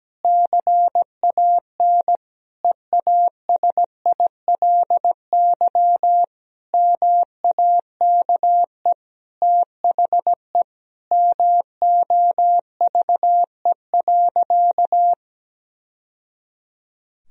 Morse code, heard as a keyed tone, about 17 wpm, 700 Hz. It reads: CAN EASILY MAKE THE MOVE.
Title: Sentences from Top 1000 Words - Review Part 1 - 17wpm